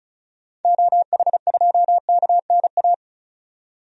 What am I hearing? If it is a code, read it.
Morse code, heard as a keyed tone, about 34 wpm, 700 Hz. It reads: OH2KNA
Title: DXCC-100 Call Signs - 6 Characters - 35wpm